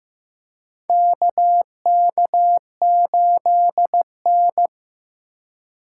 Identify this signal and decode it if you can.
Morse code, heard as a keyed tone, about 15 wpm, 690 Hz. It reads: KK8N
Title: USA Call Signs - 4 Characters - 2x Character Spacing - 15wpm